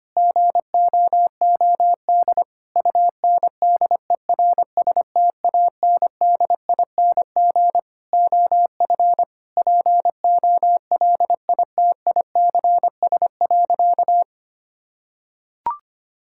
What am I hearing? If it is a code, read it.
Morse code, heard as a keyed tone, about 25 wpm, 700 Hz. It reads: GOOD UNDERSTANDING OF POLITICS.
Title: Sentences from Top 1000 Words - Review Part 6 - 25wpm